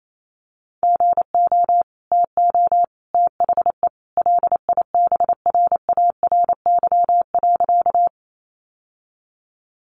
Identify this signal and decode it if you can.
Morse code, heard as a keyed tone, about 28 wpm, 700 Hz. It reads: GO TO THE LIBRARY.
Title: Sentences from Top 600 Words 28wpm